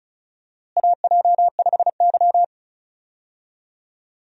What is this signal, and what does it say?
Morse code, heard as a keyed tone, about 36 wpm, 700 Hz. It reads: AJ5Y
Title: USA Call Signs - 4 Characters - Repeated 3x - 35wpm